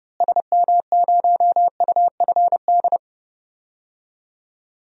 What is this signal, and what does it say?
Morse code, heard as a keyed tone, about 30 wpm, 700 Hz. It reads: SM0UFD
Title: DXCC-100 Call Signs - 6 Characters - Repeated 3x - 30wpm